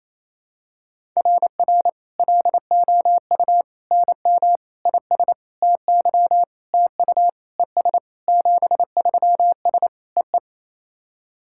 Morse code, heard as a keyed tone, about 28 wpm, 700 Hz. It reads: RR LOU NM IS TY TU ES 73S EE
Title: SOTA QSO Practice 28wpm